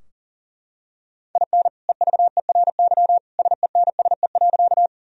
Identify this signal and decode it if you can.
Morse code, heard as a keyed tone, about 40 wpm, 700 Hz. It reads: IN EVERY SENSE.
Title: Sentences from Top 1000 Words - Review Part 4 - 40wpm